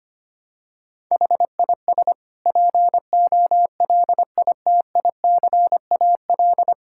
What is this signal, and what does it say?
Morse code, heard as a keyed tone, about 25 wpm, 700 Hz. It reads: HIS POLITICAL
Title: Sentences from Top 1000 Words - Review Part 6 - 25wpm